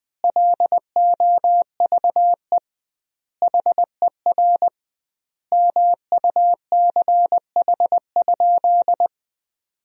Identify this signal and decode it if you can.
Morse code, heard as a keyed tone, about 21 wpm, 700 Hz. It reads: LOVE HER MUCH?
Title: 2x Word-Spacing - Sentences from Top 400 Words - 20wpm